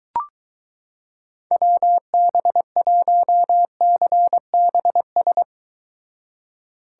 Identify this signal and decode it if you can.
Morse code, heard as a keyed tone, about 22 wpm, 700 Hz. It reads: WB1CBS